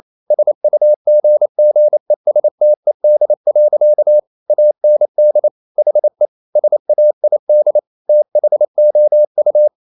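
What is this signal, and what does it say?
Morse code, heard as a keyed tone, about 28 wpm, 600 Hz. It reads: SUGGESTED. AND HE SAID THOU
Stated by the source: Morse code practice at 28 wpm, text of Ken Follett's novel Never in English, from CW med Gnister.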